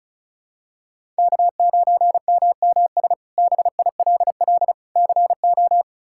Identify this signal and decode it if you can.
Morse code, heard as a keyed tone, about 34 wpm, 700 Hz. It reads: K9MMS BILL CO
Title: K1USN SST QSO Practice - 35wpm